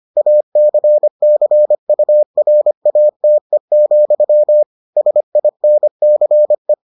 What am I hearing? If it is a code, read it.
Morse code, heard as a keyed tone, about 25 wpm, 600 Hz. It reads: ACCURATE, SINCE